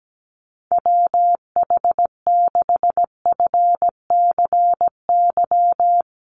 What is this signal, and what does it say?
Morse code, heard as a keyed tone, about 17 wpm, 700 Hz. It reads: WH6FCY
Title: USA Call Signs 17wpm